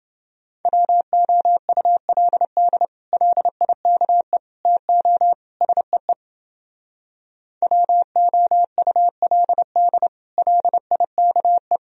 Morse code, heard as a keyed tone, about 30 wpm, 700 Hz. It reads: WOULD LIKE TO SEE WOULD LIKE